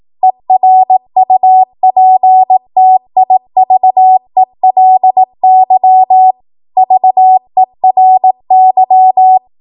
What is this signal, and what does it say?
Morse code, heard as a keyed tone, about 18 wpm, 750 Hz. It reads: ERUPTIVELY VERY
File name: Morse / 18WPM